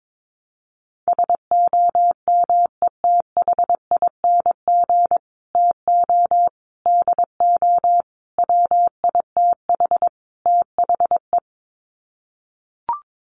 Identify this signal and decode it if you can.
Morse code, heard as a keyed tone, about 22 wpm, 700 Hz. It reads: SOMETHING TO DO WITH THE
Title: Sets of 5 Words Encore 22wpm